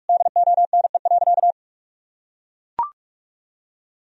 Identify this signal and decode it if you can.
Morse code, heard as a keyed tone, about 41 wpm, 700 Hz. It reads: DONE.
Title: Sentences from Top 1000 Words - Review Part 1 - 45wpm